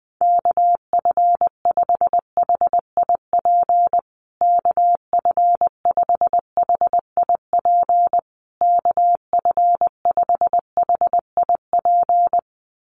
Morse code, heard as a keyed tone, about 20 wpm, 700 Hz. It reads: KF5HIP KF5HIP KF5HIP
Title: USA Call Signs Repeated 3x 20wpm